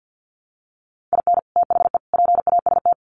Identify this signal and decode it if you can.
Morse code, heard as a keyed tone, about 50 wpm, 700 Hz. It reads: IN THE PAST
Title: Sets of 3 Words 50wpm